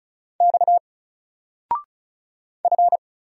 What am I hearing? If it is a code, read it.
Morse code, heard as a keyed tone, about 34 wpm, 700 Hz. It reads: X F
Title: Single Letter-Number 35wpm